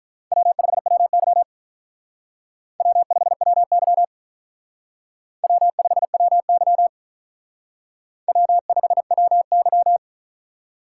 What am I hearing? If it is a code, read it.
Morse code, heard as a keyed tone, about 45 wpm, 700 Hz. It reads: W5WY W5WY W5WY W5WY